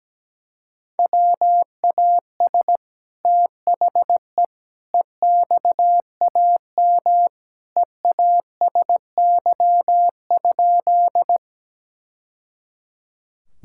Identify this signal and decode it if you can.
Morse code, heard as a keyed tone, about 17 wpm, 700 Hz. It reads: WAS THE EXAM EASY?
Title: Sentences from Top 1000 Words - Review Part 5 - 17wpm